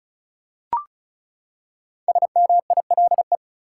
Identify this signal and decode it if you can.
Morse code, heard as a keyed tone, about 35 wpm, 700 Hz. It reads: SMILE
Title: Five Letter Words - 35wpm